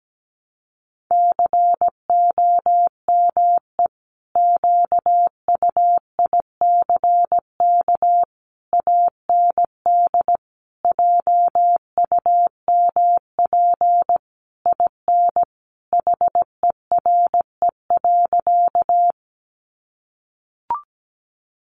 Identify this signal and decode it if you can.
Morse code, heard as a keyed tone, about 17 wpm, 700 Hz. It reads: COME QUICK AND JUMP IN HERE.
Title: Sentences from Top 1300 Words 17wpm